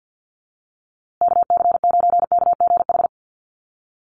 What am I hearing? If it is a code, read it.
Morse code, heard as a keyed tone, about 49 wpm, 700 Hz. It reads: KC9KGH
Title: USA Call Signs Repeated 3x 50wpm